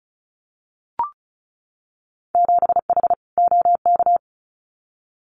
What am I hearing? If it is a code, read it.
Morse code, heard as a keyed tone, about 34 wpm, 700 Hz. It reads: 7H OK
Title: Winter Field Day Exchanges Encore 35wpm